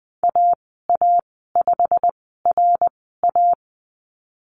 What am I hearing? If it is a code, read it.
Morse code, heard as a keyed tone, about 20 wpm, spaced as for 15 wpm, 700 Hz. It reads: AA5RA
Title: 167 Lesson 27g - Number 7 - Call Signs 20wpm